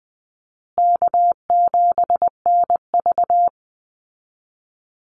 Morse code, heard as a keyed tone, about 19 wpm, 700 Hz. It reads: K7NV
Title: Easy Call Signs Repeated 3x - 20wpm